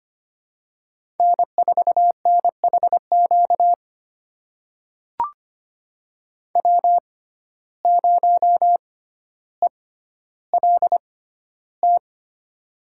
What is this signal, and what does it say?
Morse code, heard as a keyed tone, about 25 wpm, spaced as for 13 wpm, 700 Hz. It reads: N4NHQ W0ELT
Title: USA Call Signs - 5 Characters - 6x Character Spacing - 25wpm